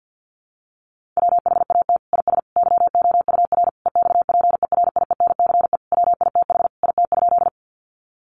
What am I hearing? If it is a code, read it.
Morse code, heard as a keyed tone, about 50 wpm, 700 Hz. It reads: WHAT IS YOUR EXPERIENCE WITH IT?
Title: Sentences from Top 600 Words 50wpm